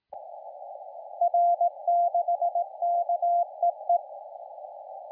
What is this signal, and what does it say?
Morse code, heard as a keyed tone, about 18 wpm, 690 Hz, with radio noise. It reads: R6KEE
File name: la4c / sac-cw-2017 / CONTEST AUDIO 1100Z